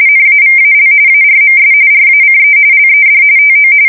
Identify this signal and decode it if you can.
RTTY, about 45.45 baud. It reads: NAVAL ARMS RACE LED TO
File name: fsk170 / battleshipnj